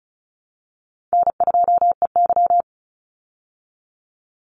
Morse code, heard as a keyed tone, about 34 wpm, 700 Hz. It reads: N2EY
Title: Call Signs Repeated 2x - 35wpm